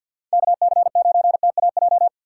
Morse code, heard as a keyed tone, about 52 wpm, 700 Hz. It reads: KC9TAJ
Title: USA Call Signs - 3x Character Spacing - 50wpm